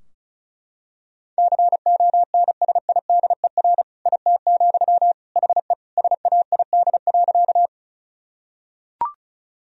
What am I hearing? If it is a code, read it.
Morse code, heard as a keyed tone, about 36 wpm, 700 Hz. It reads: CONSIDER IT, HE SAID.
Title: Sentences from Top 1000 Words - Review Part 4 - 35wpm